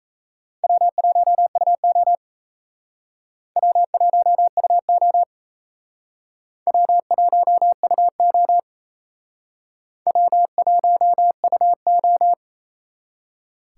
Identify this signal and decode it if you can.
Morse code, heard as a keyed tone, about 35 wpm, 700 Hz. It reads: W1UO W1UO W1UO W1UO
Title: Speed Racing - USA 1x2 Call Signs 28wpm